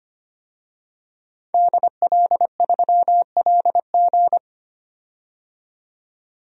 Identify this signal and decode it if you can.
Morse code, heard as a keyed tone, about 25 wpm, 700 Hz. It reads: DL3LG
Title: DXCC-100 Call Signs - 5 Characters - 25wpm